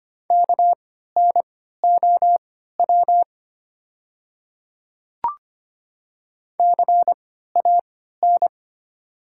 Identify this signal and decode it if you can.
Morse code, heard as a keyed tone, about 25 wpm, spaced as for 14 wpm, 700 Hz. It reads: KNOW CAN
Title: 3x Character Spacing - Top 100 Words - 25wpm